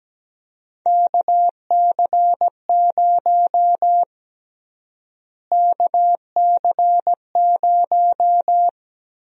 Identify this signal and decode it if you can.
Morse code, heard as a keyed tone, about 17 wpm, 700 Hz. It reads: KC0 KC0
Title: USA Call Sign Prefixes - Repeated 2x - 17wpm